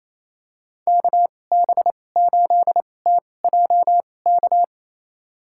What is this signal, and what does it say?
Morse code, heard as a keyed tone, about 28 wpm, spaced as for 20 wpm, 700 Hz. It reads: KB8TJK
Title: USA Call Signs - 2x Character Spacing - 28wpm